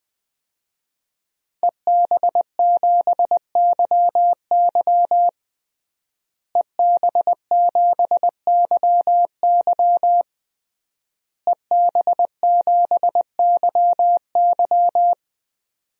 Morse code, heard as a keyed tone, about 21 wpm, 700 Hz. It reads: EB7YY EB7YY EB7YY
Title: DXCC-100 Call Signs - 5 Characters - Repeated 3x - 20wpm